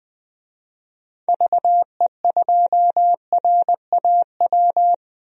Morse code, heard as a keyed tone, about 20 wpm, 700 Hz. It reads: VE2RAW
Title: DXCC-100 Call Signs - 6 Characters - Repeated 3x - 20wpm